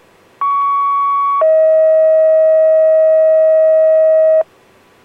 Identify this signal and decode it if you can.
Two-tone page, heard as a one-second tone, then a three-second tone, about 1120 then 640 Hz. It is heard from 0.4 to 4.4 s.